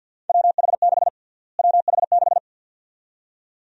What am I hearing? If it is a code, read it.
Morse code, heard as a keyed tone, about 52 wpm, 700 Hz. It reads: WH6 WH6
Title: USA Call Sign Prefixes - Repeated 2x - 50wpm